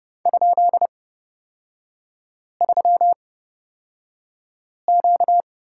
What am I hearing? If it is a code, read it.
Morse code, heard as a keyed tone, about 30 wpm, spaced as for 5 wpm, 700 Hz. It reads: ?3Q
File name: Lesson 38e - Letter q - Three Characters 30wpm